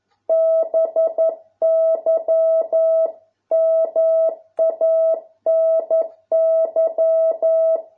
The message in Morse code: BY MANY

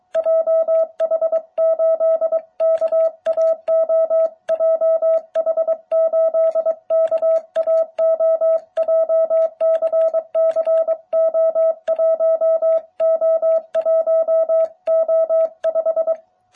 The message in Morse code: JH8KAO JH8KAO JCCO1O1O5

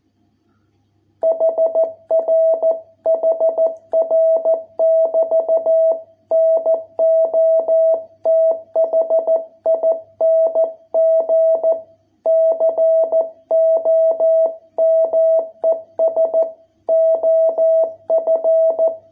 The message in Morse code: HR HR <BT> NOTHING COMES OF

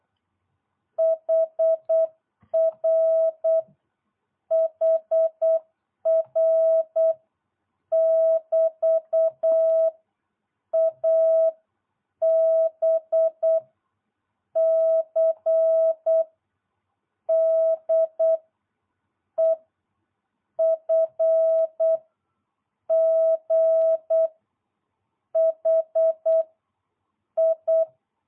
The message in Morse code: HR HR <BT> A B C D E F G H I

